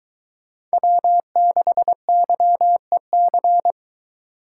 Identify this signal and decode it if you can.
Morse code, heard as a keyed tone, about 22 wpm, 700 Hz. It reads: W6YEC